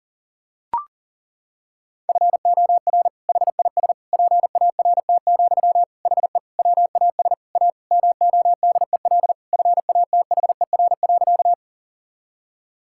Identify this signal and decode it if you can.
Morse code, heard as a keyed tone, about 37 wpm, 700 Hz. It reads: FOR HIS PART, HE WAS A MODEL FATHER.
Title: Sentences from Top 1000 Words - Review Part 5 - 40wpm